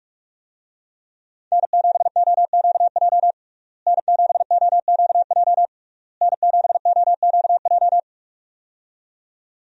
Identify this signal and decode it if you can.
Morse code, heard as a keyed tone, about 46 wpm, 700 Hz. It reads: N7OQJ N7OQJ N7OQJ
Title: USA Call Signs - 5 Characters - Repeated 3x - 45wpm